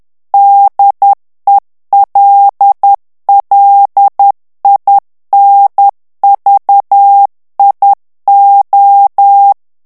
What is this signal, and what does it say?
Morse code, heard as a keyed tone, about 10 wpm, 800 Hz. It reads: DELLINVIO
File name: ITALIEN texte clair 2 vitesse 10 mots minute